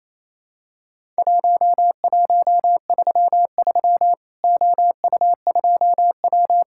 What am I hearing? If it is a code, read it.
Morse code, heard as a keyed tone, about 28 wpm, 700 Hz. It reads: 1133 OU2W
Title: Worked All Europe DX Contest - QTC Exchange - 28wpm